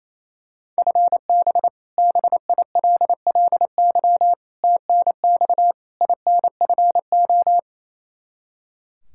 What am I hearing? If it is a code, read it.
Morse code, heard as a keyed tone, about 28 wpm, 700 Hz. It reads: FB BILLY TNX INFO
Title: K1USN SST QSO Practice 28wpm